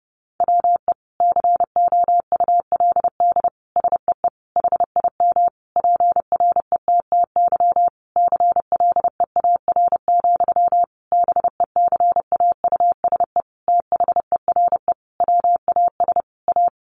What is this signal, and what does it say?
Morse code, heard as a keyed tone, about 30 wpm, 700 Hz. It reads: WE COULD SEE HIM PRETTY CLEAR, BECAUSE THERE WAS A
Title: Sentences from Top 1000 Words 30wpm